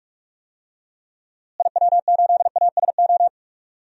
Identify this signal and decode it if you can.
Morse code, heard as a keyed tone, about 46 wpm, 690 Hz. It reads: IW8ASO
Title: DXCC-100 Call Signs - 6 Characters - Repeated 2x - 45wpm